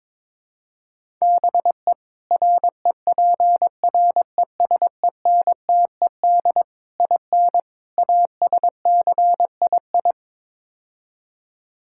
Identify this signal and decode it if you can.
Morse code, heard as a keyed tone, about 22 wpm, 700 Hz. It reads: BE REPRESENTED IN ASCII
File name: practice / 22wpm / Build A Phrase - 5-Word Phrases - 22wpm